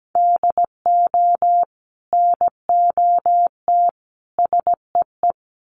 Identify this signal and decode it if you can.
Morse code, heard as a keyed tone, about 17 wpm, 700 Hz. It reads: DO NOT SEE
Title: Sets of 3 Words 17wpm